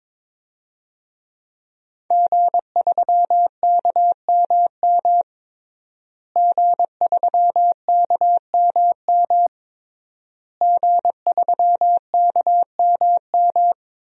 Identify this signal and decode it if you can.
Morse code, heard as a keyed tone, about 22 wpm, 700 Hz. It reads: G3KMM G3KMM G3KMM
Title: DXCC-100 Call Signs - 5 Characters - Repeated 3x - 22wpm